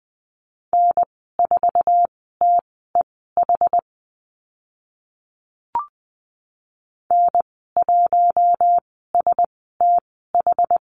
Morse code, heard as a keyed tone, about 20 wpm, spaced as for 14 wpm, 700 Hz. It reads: N4TEH N1STH
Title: 076 Lesson 13g - Letter L - Call Signs 20wpm